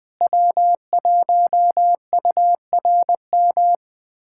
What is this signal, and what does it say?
Morse code, heard as a keyed tone, about 21 wpm, 700 Hz. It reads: W1URM
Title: USA Callsign DE Callsign - 5 Characters - 20wpm